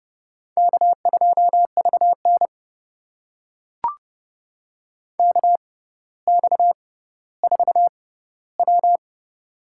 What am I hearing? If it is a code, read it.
Morse code, heard as a keyed tone, about 30 wpm, spaced as for 15 wpm, 700 Hz. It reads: K2VN KX4W
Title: USA Call Signs - 4 Characters - 6x Character Spacing - 30wpm